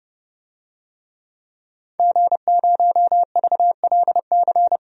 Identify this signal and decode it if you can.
Morse code, heard as a keyed tone, about 30 wpm, 700 Hz. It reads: G0VLC
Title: DXCC-100 Call Signs - 5 Characters - Repeated 3x - 30wpm